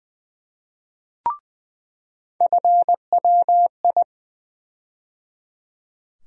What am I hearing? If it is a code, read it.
Morse code, heard as a keyed tone, about 20 wpm, 700 Hz. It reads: FWI
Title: USA Call Sign Postfixes - 3 Letters 20wpm